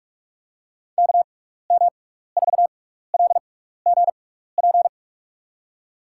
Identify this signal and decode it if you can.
Morse code, heard as a keyed tone, about 46 wpm, 700 Hz. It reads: K M 4 L G P
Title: USA Call Signs - 6 Characters - 6x Character Spacing - 45wpm